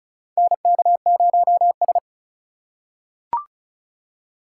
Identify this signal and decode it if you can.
Morse code, heard as a keyed tone, about 34 wpm, 700 Hz. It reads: NK0S
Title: USA Call Signs - 4 Characters - 4x Character Spacing - 35wpm